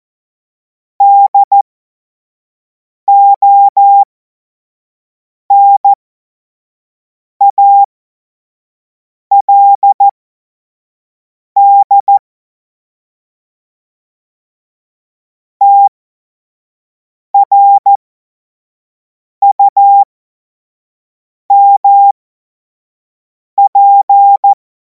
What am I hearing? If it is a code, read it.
Morse code, heard as a keyed tone, about 14 wpm, spaced as for 5 wpm, 800 Hz. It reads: DONALD TRUMP
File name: Morsekurs / Witze / Witz 19. Mai 2026 25 F70